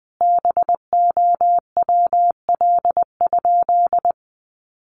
Morse code, heard as a keyed tone, about 20 wpm, 700 Hz. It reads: BOWL?
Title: Sentences from Top 1200 Words 20wpm